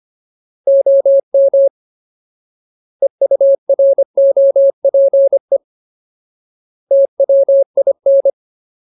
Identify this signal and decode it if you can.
Morse code, heard as a keyed tone, about 25 wpm, 550 Hz. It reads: OM EUROPE TWIN
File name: Ultimate Word List / 25WPM Ultimate